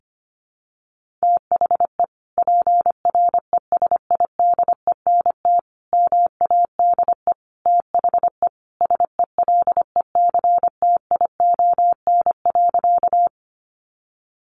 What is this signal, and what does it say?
Morse code, heard as a keyed tone, about 25 wpm, 700 Hz. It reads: THE PRESIDENT MADE THE SELECTION.
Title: Sentences from Top 1400 Words 25wpm